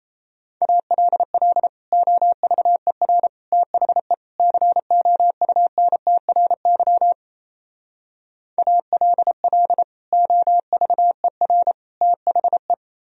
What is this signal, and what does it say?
Morse code, heard as a keyed tone, about 36 wpm, 700 Hz. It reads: ALL OVER THE COUNTRY ALL OVER THE